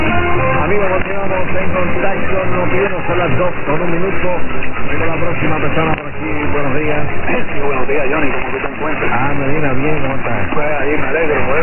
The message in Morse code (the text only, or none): none